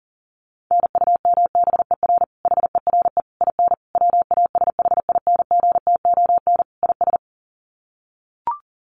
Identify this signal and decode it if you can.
Morse code, heard as a keyed tone, about 40 wpm, 700 Hz. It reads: NUMBER HERE IN WASHINGTON IS
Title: Sets of 5 Words Encore2 40wpm